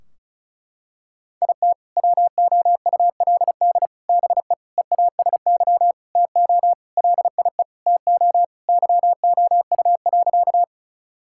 Morse code, heard as a keyed tone, about 34 wpm, 700 Hz. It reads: IT WOULD BE EASY TO LIE TO YOU.
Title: Sentences from Top 1600 Words 35wpm